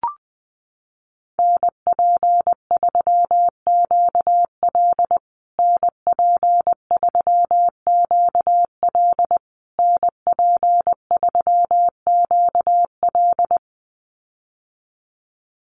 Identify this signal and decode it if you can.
Morse code, heard as a keyed tone, about 20 wpm, 700 Hz. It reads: NP3QL NP3QL NP3QL
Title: USA Call Signs Repeated 3x 20wpm